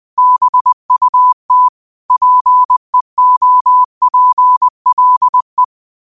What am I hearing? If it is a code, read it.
Morse code, heard as a keyed tone, about 20 wpm, 1000 Hz. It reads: BUT PEOPLE
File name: qotdfiles / 20wpm